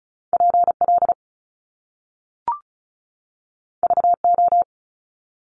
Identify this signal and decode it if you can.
Morse code, heard as a keyed tone, about 36 wpm, 700 Hz. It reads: PL VO
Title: Top Two Letter Combinations in Words 35wpm